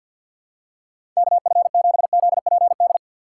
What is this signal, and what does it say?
Morse code, heard as a keyed tone, about 51 wpm, 690 Hz. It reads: KF7ZPD
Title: USA Call Signs - 4x Character Spacing - 50wpm